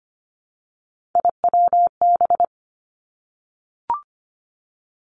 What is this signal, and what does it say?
Morse code, heard as a keyed tone, about 25 wpm, 700 Hz. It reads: IWB
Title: Three Challenging Letter-Number Combinations 25wpm